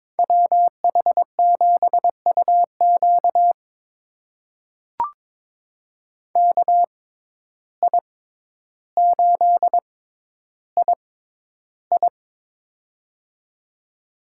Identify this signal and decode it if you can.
Morse code, heard as a keyed tone, about 22 wpm, spaced as for 11 wpm, 700 Hz. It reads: WH7UQ KI8II
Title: USA Call Signs - 5 Characters - 6x Character Spacing - 22wpm